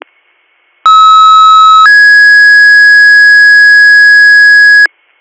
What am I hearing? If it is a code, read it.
Two-tone page, heard as a one-second tone, then a three-second tone, about 1290 then 1650 Hz. It is heard from 0.9 to 4.9 s.